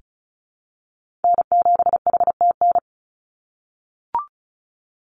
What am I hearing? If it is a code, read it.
Morse code, heard as a keyed tone, about 34 wpm, 700 Hz. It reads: N7HTN
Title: USA Call Signs 35wpm